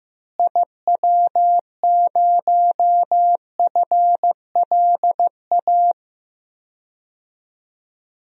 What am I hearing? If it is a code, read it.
Morse code, heard as a keyed tone, about 15 wpm, 700 Hz. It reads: IW0FLA
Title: DXCC-100 Call Signs - 6 Characters - Repeated 3x - 15wpm